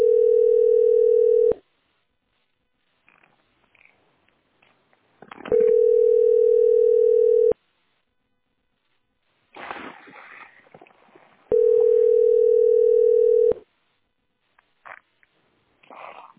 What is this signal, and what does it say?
Ringback tone (North America), 440 + 480 Hz.